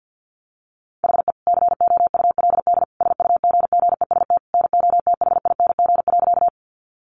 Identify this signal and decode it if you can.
Morse code, heard as a keyed tone, about 50 wpm, 700 Hz. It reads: HE COULD SUGGEST NOTHING.
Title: Sentences from Top 1300 Words 50wpm